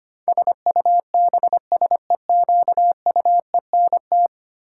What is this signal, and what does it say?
Morse code, heard as a keyed tone, about 25 wpm, 700 Hz. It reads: SUBSEQUENT
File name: Ten Letter Words - 25wpm